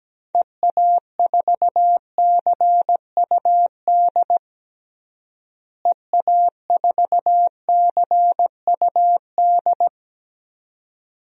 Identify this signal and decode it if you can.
Morse code, heard as a keyed tone, about 17 wpm, 700 Hz. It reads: EA4CUD EA4CUD
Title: DXCC-100 Call Signs - 6 Characters - Repeated 2x - 17wpm